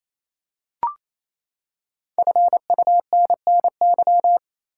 Morse code, heard as a keyed tone, about 28 wpm, 700 Hz. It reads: FUNNY
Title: Five Letter Words - 28wpm